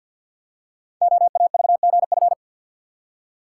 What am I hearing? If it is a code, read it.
Morse code, heard as a keyed tone, about 46 wpm, 700 Hz. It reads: OAVGF
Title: Code Groups - 5 Letters - 50wpm